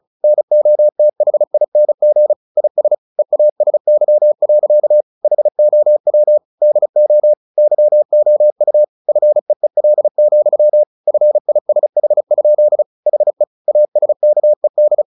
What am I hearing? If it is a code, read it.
Morse code, heard as a keyed tone, about 34 wpm, 600 Hz. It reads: NOTHING IS EASY. HOW DO YOU FEEL, FISH? HE ASKED
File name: the old man and the sea 2 35wpm